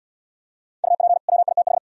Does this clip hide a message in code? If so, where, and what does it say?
Yes: Morse code from 0.8 to 1.8 s, heard as a keyed tone, about 69 wpm, 700 Hz. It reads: UP WITH